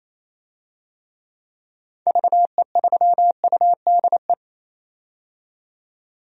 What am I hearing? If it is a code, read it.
Morse code, heard as a keyed tone, about 28 wpm, 700 Hz. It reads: VE3UDE
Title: DXCC-100 Call Signs - 6 Characters - 28wpm